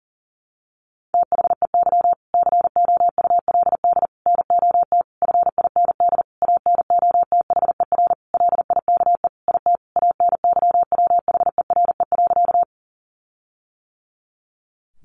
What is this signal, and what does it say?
Morse code, heard as a keyed tone, about 40 wpm, 700 Hz. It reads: THEY COULD NOT FIND ANOTHER LIKE IT ANYWHERE.
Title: Sentences from Top 1400 Words 40wpm